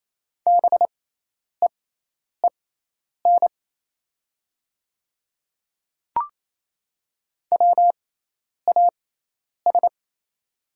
Morse code, heard as a keyed tone, about 28 wpm, spaced as for 10 wpm, 700 Hz. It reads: BEEN WAS
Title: 6x Character Spacing - Top 200 Words - 28wpm